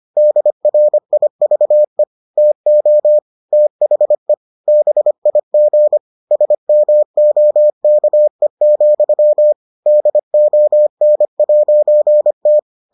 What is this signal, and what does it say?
Morse code, heard as a keyed tone, about 25 wpm, 600 Hz. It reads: DRIVE TO THE BIG SMOKE, DON'T